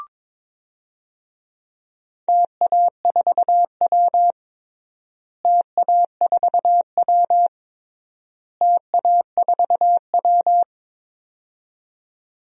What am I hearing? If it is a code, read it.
Morse code, heard as a keyed tone, about 22 wpm, 700 Hz. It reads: TA4W TA4W TA4W
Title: DXCC-100 Call Signs - 4 Characters - Repeated 3x - 22wpm